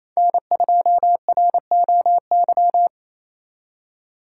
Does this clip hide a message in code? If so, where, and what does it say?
Yes: Morse code from 0.2 to 2.9 s, heard as a keyed tone, about 35 wpm, 700 Hz. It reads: N2ROY